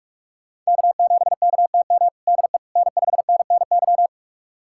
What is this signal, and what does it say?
Morse code, heard as a keyed tone, about 46 wpm, 700 Hz. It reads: K7KTM DE N5NNY
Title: USA Callsign DE Callsign 45wpm